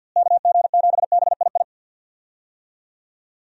Morse code, heard as a keyed tone, about 52 wpm, 700 Hz. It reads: KG7BII
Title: USA Call Signs - 6 Characters - Repeated 2x - 50wpm